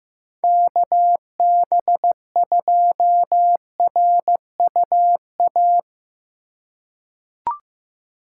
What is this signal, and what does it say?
Morse code, heard as a keyed tone, about 15 wpm, 700 Hz. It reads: KB2RUA
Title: USA Call Signs - 2x Character Spacing - 15wpm